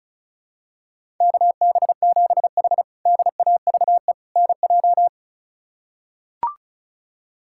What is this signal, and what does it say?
Morse code, heard as a keyed tone, about 34 wpm, 700 Hz. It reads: KB7H DAVE NJ
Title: K1USN SST QSO Practice - 35wpm